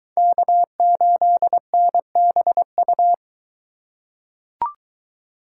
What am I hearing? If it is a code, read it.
Morse code, heard as a keyed tone, about 23 wpm, 700 Hz. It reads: K8NBU